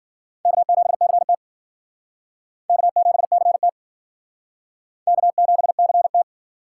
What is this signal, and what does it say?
Morse code, heard as a keyed tone, about 50 wpm, 700 Hz. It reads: K7CT K7CT K7CT
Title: Speed Racing - USA 1x2 Call Signs 40wpm